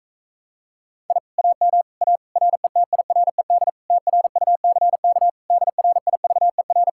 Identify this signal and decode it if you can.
Morse code, heard as a keyed tone, about 45 wpm, 700 Hz. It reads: I AM A RETIRED TRUCK DRIVER